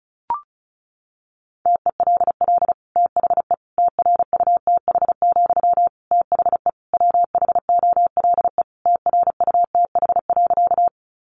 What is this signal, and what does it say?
Morse code, heard as a keyed tone, about 34 wpm, 700 Hz. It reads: TELL THE TRUTH, THE WHOLE TRUTH.
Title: Sentences from Top 800 Words 35wpm